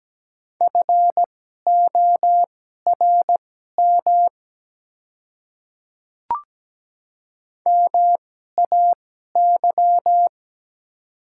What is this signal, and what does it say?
Morse code, heard as a keyed tone, about 17 wpm, spaced as for 12 wpm, 700 Hz. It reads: FORM MAY
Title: 2x Character Spacing - Top 300 Words - 17wpm